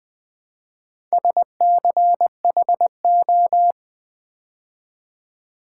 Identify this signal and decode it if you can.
Morse code, heard as a keyed tone, about 20 wpm, 700 Hz. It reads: SCHO
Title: Build A Word - 4 Letters At A Time - 11-Letter Words - 20wpm